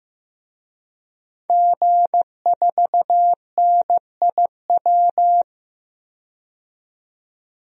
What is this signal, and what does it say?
Morse code, heard as a keyed tone, about 15 wpm, 700 Hz. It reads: G4NIW
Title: DXCC-100 Call Signs - 5 Characters - 15wpm